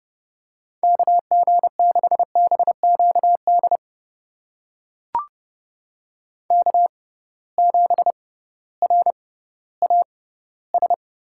Morse code, heard as a keyed tone, about 30 wpm, spaced as for 16 wpm, 700 Hz. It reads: KG6BQD K7RAS
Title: USA Call Signs - 6x Character Spacing - 30wpm